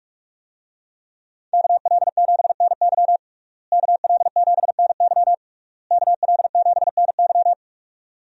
Morse code, heard as a keyed tone, about 42 wpm, 700 Hz. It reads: KL7NY KL7NY KL7NY
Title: USA Call Signs - 5 Characters - Repeated 3x - 45wpm